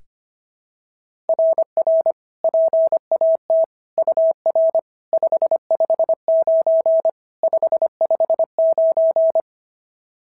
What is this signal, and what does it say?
Morse code, heard as a keyed tone, about 25 wpm, 690 Hz. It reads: RR PAT UR 559 559
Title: SOTA QSO Practice 25wpm